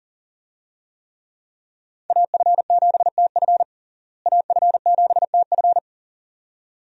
Morse code, heard as a keyed tone, about 40 wpm, 700 Hz. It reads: AF7TF AF7TF
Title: USA Call Signs - 5 Characters - Repeated 2x - 40wpm